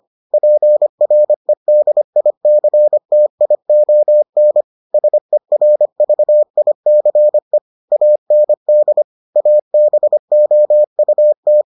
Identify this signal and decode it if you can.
Morse code, heard as a keyed tone, about 25 wpm, 600 Hz. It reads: PREDICTION SERVICE AND ABOUT